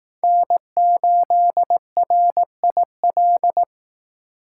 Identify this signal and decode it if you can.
Morse code, heard as a keyed tone, about 18 wpm, 700 Hz. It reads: N8RIL
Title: Speed Racing - USA 1x3 Call Signs 15wpm